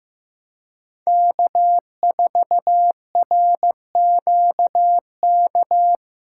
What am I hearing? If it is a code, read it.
Morse code, heard as a keyed tone, about 18 wpm, 700 Hz. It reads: K4RQK
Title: Speed Racing - USA 1x3 Call Signs 15wpm